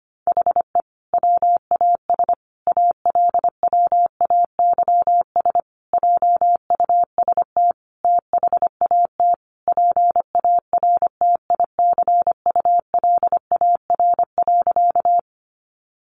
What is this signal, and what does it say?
Morse code, heard as a keyed tone, about 25 wpm, 700 Hz. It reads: HE WAS ALWAYS JUST THAT PARTICULAR.
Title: Sentences from Top 600 Words 25wpm